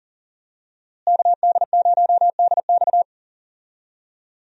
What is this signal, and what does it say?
Morse code, heard as a keyed tone, about 40 wpm, 700 Hz. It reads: KD0DX
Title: USA Call Signs - 5 Characters - Repeated 2x - 40wpm